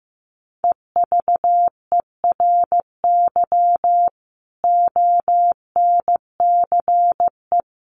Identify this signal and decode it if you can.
Morse code, heard as a keyed tone, about 15 wpm, 700 Hz. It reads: EVERY ONCE
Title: Sets of 5 Words 15wpm